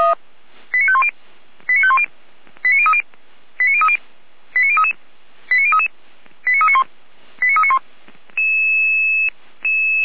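Five-tone selective call (ZVEI), heard as quick runs of five tones, tones of about 70 ms. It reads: 89510 89510 89030 89030 89033 89033 89391 89391
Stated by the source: Five-tone ZVEI selective call received on an SDR.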